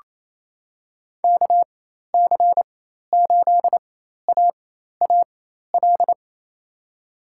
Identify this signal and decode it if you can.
Morse code, heard as a keyed tone, about 28 wpm, spaced as for 13 wpm, 700 Hz. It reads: KC8AAL
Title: USA Call Signs - 6 Characters - 4x Character Spacing - 28wpm